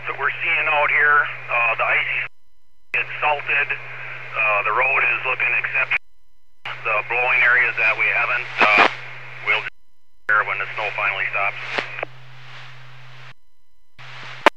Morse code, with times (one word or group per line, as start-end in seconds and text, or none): none